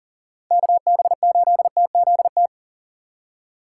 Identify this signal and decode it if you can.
Morse code, heard as a keyed tone, about 40 wpm, 700 Hz. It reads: KB8TZT
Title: USA Call Signs - 6 Characters - 2x Character Spacing - 40wpm